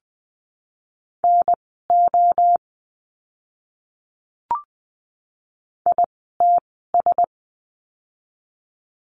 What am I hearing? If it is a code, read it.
Morse code, heard as a keyed tone, about 20 wpm, spaced as for 14 wpm, 700 Hz. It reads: NO ITS
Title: 061 Lesson 11f - Letter h - Words 20wpm